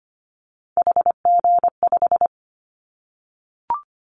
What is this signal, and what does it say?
Morse code, heard as a keyed tone, about 25 wpm, 700 Hz. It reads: HG5
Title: Callsign Prefixes for Top 25 DXCC - Repeated 3x - 25wpm